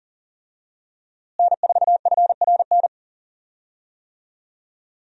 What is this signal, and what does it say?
Morse code, heard as a keyed tone, about 40 wpm, 700 Hz. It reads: N4FRN
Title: USA Call Signs - 5 Characters - Repeated 3x - 40wpm